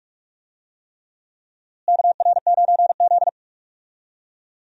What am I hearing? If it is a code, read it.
Morse code, heard as a keyed tone, about 46 wpm, 700 Hz. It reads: KR9Z
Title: USA Call Signs - 4 Characters - Repeated 3x - 45wpm